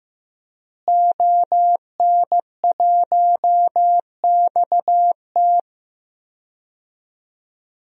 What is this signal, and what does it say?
Morse code, heard as a keyed tone, about 15 wpm, 700 Hz. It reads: ON1XT
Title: DXCC-100 Call Signs - 5 Characters - Repeated 2x - 15wpm